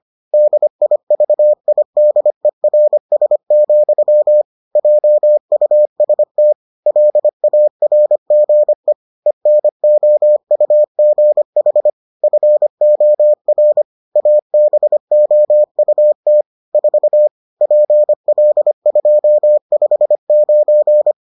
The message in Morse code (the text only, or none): DIVIDERS, JUST LARGE ENOUGH FOR ABOUT 4 PL259